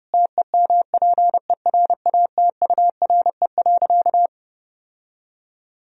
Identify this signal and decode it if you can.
Morse code, heard as a keyed tone, about 32 wpm, 700 Hz. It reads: TEMPERATURE.
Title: Sentences from Top 1000 Words - Review Part 4 - 30wpm